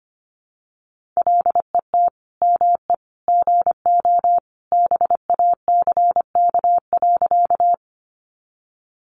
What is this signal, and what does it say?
Morse code, heard as a keyed tone, about 25 wpm, 700 Hz. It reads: LET ME GO BACK.
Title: Sentences from Top 300 Words 25wpm